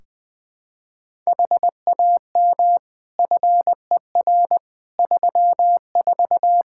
Morse code, heard as a keyed tone, about 20 wpm, 700 Hz. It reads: HAM FER 34
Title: Practice QSO - Exchange 3 - More Info - 20wpm